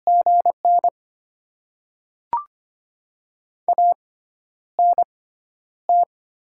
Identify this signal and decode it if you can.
Morse code, heard as a keyed tone, about 25 wpm, spaced as for 11 wpm, 700 Hz. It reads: GN ANT
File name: Top 20 QSO Elements - 6x Character Spacing - 25wpm